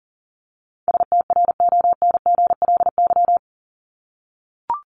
Morse code, heard as a keyed tone, about 40 wpm, 700 Hz. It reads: STRONGLY
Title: LY Suffix - Top 100 Words with Base Words 40wpm